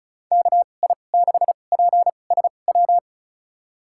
Morse code, heard as a keyed tone, about 34 wpm, spaced as for 25 wpm, 690 Hz. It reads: KI6PSW
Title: USA Call Signs - 6 Characters - 2x Character Spacing - 35wpm